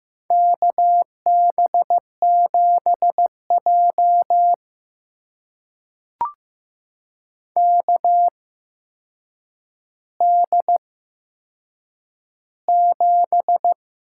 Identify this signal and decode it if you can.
Morse code, heard as a keyed tone, about 15 wpm, spaced as for 7 wpm, 700 Hz. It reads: KB7J KD7
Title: USA Call Signs - 8x Character Spacing - 15wpm